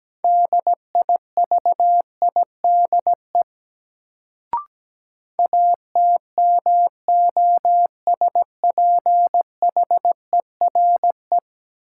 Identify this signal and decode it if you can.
Morse code, heard as a keyed tone, about 17 wpm, 700 Hz. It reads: DIVIDE ATMOSPHERE
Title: Sentences from Top 1600 Words - New Word Drill - 17wpm